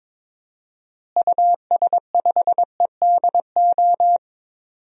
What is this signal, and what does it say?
Morse code, heard as a keyed tone, about 23 wpm, 700 Hz. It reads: US5EDO
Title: DXCC-100 Call Signs - 6 Characters - Repeated 2x - 22wpm